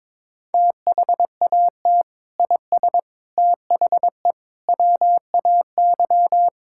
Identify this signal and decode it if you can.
Morse code, heard as a keyed tone, about 22 wpm, 700 Hz. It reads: THAT IS THE WAY
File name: practice / 22wpm / Sentences with 3-Word Phrases - 22wpm